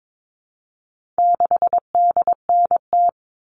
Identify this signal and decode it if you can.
Morse code, heard as a keyed tone, about 22 wpm, 700 Hz. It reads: 6DNT
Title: All Cut Numbers - 4-Digit 22wpm